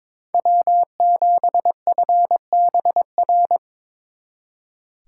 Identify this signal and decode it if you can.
Morse code, heard as a keyed tone, about 25 wpm, 700 Hz. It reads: W7FBR
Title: Speed Racing - USA 1x3 Call Signs 22wpm